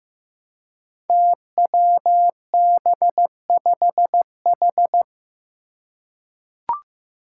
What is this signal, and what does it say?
Morse code, heard as a keyed tone, about 15 wpm, 700 Hz. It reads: TWB5H
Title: Five Challenging Letter-Number Combinations 15wpm